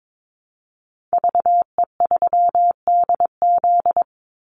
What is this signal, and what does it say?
Morse code, heard as a keyed tone, about 22 wpm, 700 Hz. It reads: VE3DZ
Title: Call Signs Repeated 3x - 22wpm